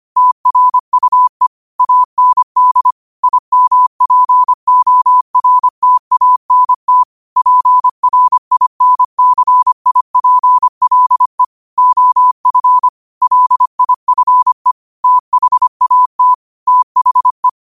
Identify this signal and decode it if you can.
Morse code, heard as a keyed tone, about 25 wpm, 1000 Hz. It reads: TRUE AND IMPORTANT PRINCIPLE OF LIFE THAT THE
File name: qotdfiles / 25wpm